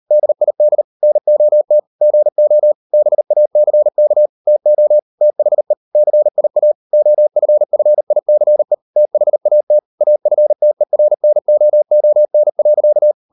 Morse code, heard as a keyed tone, about 40 wpm, 600 Hz. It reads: DID NOT GO BACK TO THE CIA OFFICE THAT AFTERNOON.